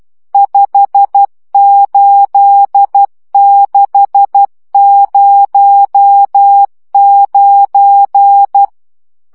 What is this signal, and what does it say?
Morse code, heard as a keyed tone, about 12 wpm, 800 Hz. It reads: 58609
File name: morze vizsga 2011-3 10